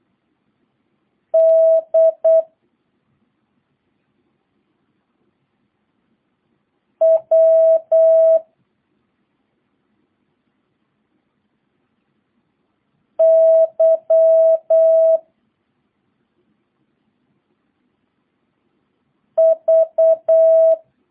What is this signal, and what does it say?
Morse code, heard as a keyed tone, about 8 wpm, 650 Hz. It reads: D W Y V